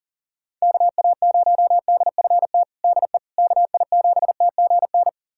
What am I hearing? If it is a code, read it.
Morse code, heard as a keyed tone, about 40 wpm, 700 Hz. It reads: KA0DFT DE KI7TGN
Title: USA Callsign DE Callsign - 6 Characters - 40wpm